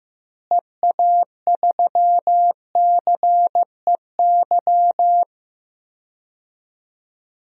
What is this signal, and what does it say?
Morse code, heard as a keyed tone, about 15 wpm, 700 Hz. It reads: EA3CEY
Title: DXCC-100 Call Signs - 6 Characters - Repeated 2x - 15wpm